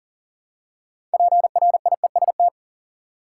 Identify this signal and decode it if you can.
Morse code, heard as a keyed tone, about 40 wpm, 700 Hz. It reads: PRIEST
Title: Six Letter Words - 40wpm